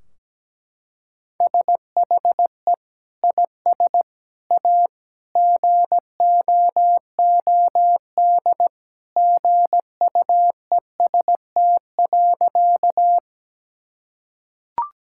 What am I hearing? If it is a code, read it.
Morse code, heard as a keyed tone, about 17 wpm, 700 Hz. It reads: SHE IS A GOOD GUEST.
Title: Sentences from Top 1600 Words 17wpm